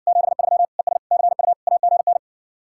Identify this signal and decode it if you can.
Morse code, heard as a keyed tone, about 61 wpm, 700 Hz. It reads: 73 ES CU AGN